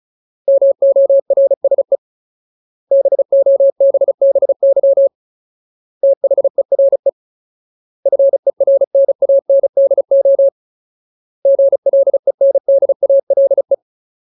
Morse code, heard as a keyed tone, about 34 wpm, 550 Hz. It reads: MORSE BOBBY THERE FERNANDO GLENDALE
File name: Ultimate Word List / 35WPM Ultimate